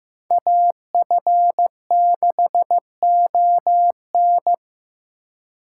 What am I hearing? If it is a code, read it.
Morse code, heard as a keyed tone, about 16 wpm, 700 Hz. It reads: AF6ON
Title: USA Call Signs - 5 Characters - 2x Character Spacing - 15wpm